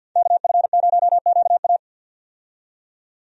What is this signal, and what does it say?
Morse code, heard as a keyed tone, about 50 wpm, 700 Hz. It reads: KF0QA
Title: USA Call Signs - 5 Characters - 50wpm